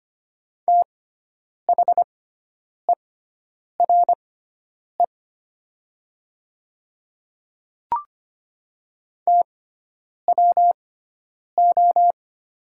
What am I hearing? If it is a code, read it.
Morse code, heard as a keyed tone, about 25 wpm, spaced as for 9 wpm, 700 Hz. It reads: THERE TWO
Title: 6x Character Spacing - Top 300 Words - 25wpm